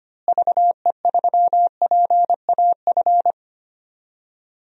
Morse code, heard as a keyed tone, about 25 wpm, 700 Hz. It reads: VE3PAF
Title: DXCC-100 Call Signs - 6 Characters - Repeated 3x - 25wpm